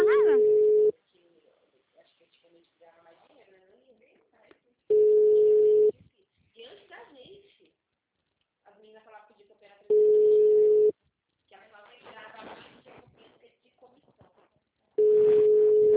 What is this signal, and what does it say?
Ringback tone (Europe, 425 Hz), with speech.
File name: out- -2002- - - .44699